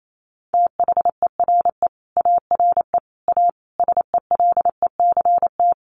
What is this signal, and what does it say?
Morse code, heard as a keyed tone, about 28 wpm, 700 Hz. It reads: THERE ARE A SELECT
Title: Sentences from Top 1300 Words 28wpm